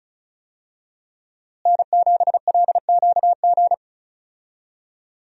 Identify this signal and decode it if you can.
Morse code, heard as a keyed tone, about 36 wpm, 700 Hz. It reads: N7LQG
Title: USA Call Signs - 5 Characters - Repeated 3x - 35wpm